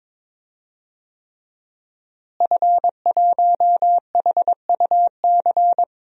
Morse code, heard as a keyed tone, about 22 wpm, 700 Hz. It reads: F1HUC
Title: DXCC-100 Call Signs - 5 Characters - Repeated 3x - 22wpm